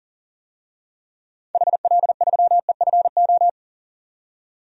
Morse code, heard as a keyed tone, about 40 wpm, 700 Hz. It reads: HL3EFO
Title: DXCC-100 Call Signs - 6 Characters - 40wpm